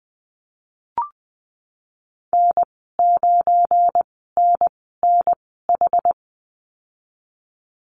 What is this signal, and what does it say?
Morse code, heard as a keyed tone, about 20 wpm, spaced as for 15 wpm, 700 Hz. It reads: N9NNH
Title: 186 Lesson 30g - Letter b - Call Signs 20wpm